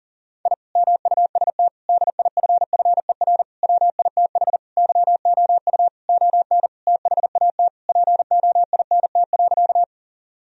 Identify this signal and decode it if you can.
Morse code, heard as a keyed tone, about 40 wpm, 700 Hz. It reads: I MUST DIFFER WITH YOU ON THAT POINT.
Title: Sentences from Top 1000 Words - Review Part 1 - 40wpm